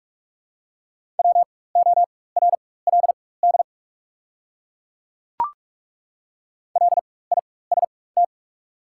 Morse code, heard as a keyed tone, about 42 wpm, spaced as for 20 wpm, 700 Hz. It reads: WORLD LIST
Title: 4x Character Spacing - Top 500 Words - 45wpm